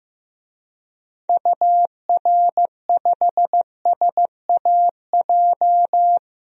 Morse code, heard as a keyed tone, about 15 wpm, 700 Hz. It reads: UR5SAJ
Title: DXCC-100 Call Signs - 6 Characters - Repeated 3x - 15wpm